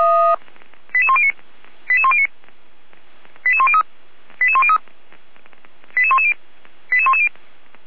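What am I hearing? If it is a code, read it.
Five-tone selective call (ZVEI), heard as quick runs of five tones, tones of about 70 ms. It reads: 88109 88109 88103 88103 88110 88110
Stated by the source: ZVEI five-tone selective call received on SDR, code 88405.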